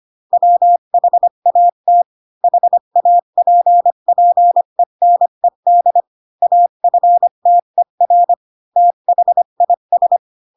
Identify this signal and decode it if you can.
Morse code, heard as a keyed tone, about 25 wpm, 700 Hz. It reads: WHAT HAPPENED AFTER THIS
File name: Pinocchio / Pinocchio ch3 20 25 ik1hge